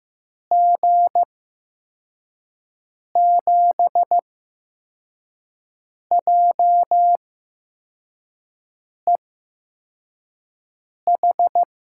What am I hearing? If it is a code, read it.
Morse code, heard as a keyed tone, about 15 wpm, spaced as for 4 wpm, 700 Hz. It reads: G7JEH
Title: USA Call Signs - 6 Characters - 8x Character Spacing - 15wpm